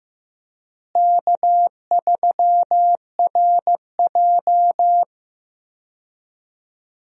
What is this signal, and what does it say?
Morse code, heard as a keyed tone, about 15 wpm, 700 Hz. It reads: K3RJ
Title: USA Call Signs - 4 Characters - Repeated 3x - 15wpm